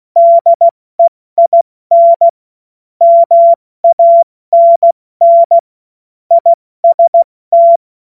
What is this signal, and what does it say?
Morse code, heard as a keyed tone, about 16 wpm, 680 Hz. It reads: DEIN MANN IST
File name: Morsekurs / Witze / Witz 12 März 2024 70 F80